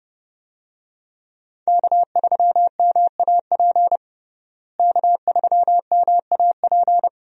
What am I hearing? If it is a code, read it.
Morse code, heard as a keyed tone, about 30 wpm, 700 Hz. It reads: K3MAP K3MAP
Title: USA Call Signs - 5 Characters - Repeated 2x - 30wpm